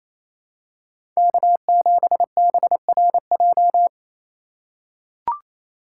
Morse code, heard as a keyed tone, about 28 wpm, 700 Hz. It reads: K7BRJ
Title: USA Call Signs - 5 Characters - 4x Character Spacing - 28wpm